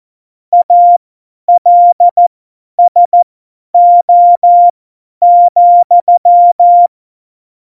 Morse code, heard as a keyed tone, about 14 wpm, spaced as for 10 wpm, 700 Hz. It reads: ALSO,
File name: Morsekurs / Witze / Witz 28. November 2017, 50 F70